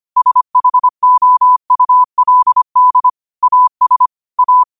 Morse code, heard as a keyed tone, about 25 wpm, 1000 Hz. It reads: SHOULD AS A
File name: qotdfiles / 25wpm